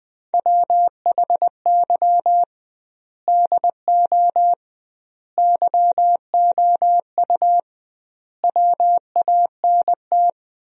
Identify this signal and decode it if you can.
Morse code, heard as a keyed tone, about 20 wpm, 700 Hz. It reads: WHY DO YOU WANT
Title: 2x Word-Spacing - Sentences from Top 200 Words - 20wpm